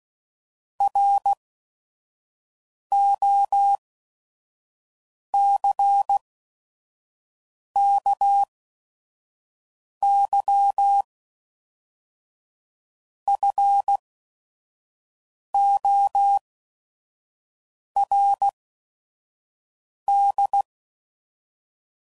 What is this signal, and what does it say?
Morse code, heard as a keyed tone, about 16 wpm, spaced as for 5 wpm, 800 Hz. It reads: ROCKYFORD
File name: CW Code Groups @5WPM / fromSM's / Cities